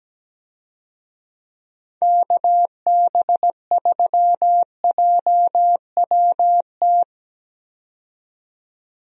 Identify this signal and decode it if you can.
Morse code, heard as a keyed tone, about 17 wpm, 700 Hz. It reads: KB3JWT
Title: USA Call Signs - 6 Characters - 17wpm